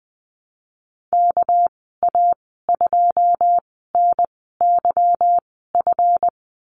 Morse code, heard as a keyed tone, about 19 wpm, spaced as for 14 wpm, 700 Hz. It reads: KA2NYF
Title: 207 Lesson 33g - Letter j - Call Signs 20wpm